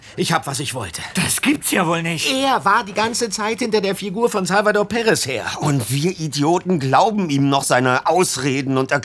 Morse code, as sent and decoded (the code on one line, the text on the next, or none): none